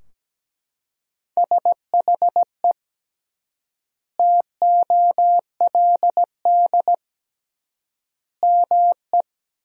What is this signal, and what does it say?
Morse code, heard as a keyed tone, about 17 wpm, 700 Hz. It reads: SHE TOLD ME
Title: 3x Word-Spacing - Sentences from Top 400 Words - 17wpm